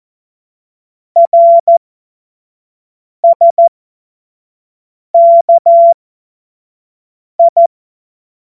Morse code, heard as a keyed tone, about 14 wpm, spaced as for 5 wpm, 680 Hz. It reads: RSKI
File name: Morsekurs